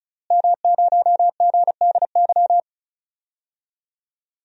Morse code, heard as a keyed tone, about 34 wpm, 690 Hz. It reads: M0GDY
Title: DXCC-100 Call Signs - 5 Characters - Repeated 3x - 35wpm